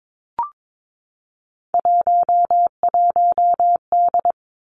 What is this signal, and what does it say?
Morse code, heard as a keyed tone, about 22 wpm, 700 Hz. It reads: 11D
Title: Three Challenging Letter-Number Combinations 22wpm